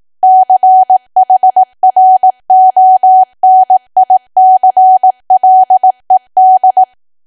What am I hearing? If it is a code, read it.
Morse code, heard as a keyed tone, about 18 wpm, 750 Hz. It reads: CHRONICLED
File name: Archive / 18 WPM / 18WPM